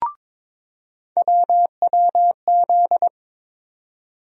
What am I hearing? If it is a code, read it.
Morse code, heard as a keyed tone, about 22 wpm, 700 Hz. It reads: WWZ
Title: USA Call Sign Postfixes - 3 Letters 22wpm